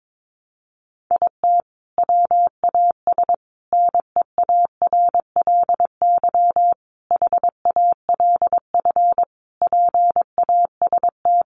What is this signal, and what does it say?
Morse code, heard as a keyed tone, about 22 wpm, 700 Hz. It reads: IT WAS NEARLY HALF PAST
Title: Sentences from Top 1300 Words 22wpm